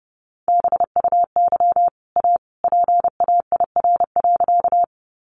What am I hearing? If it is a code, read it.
Morse code, heard as a keyed tone, about 30 wpm, 700 Hz. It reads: BUY A PAIR.
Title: Sentences from Top 1200 Words 30wpm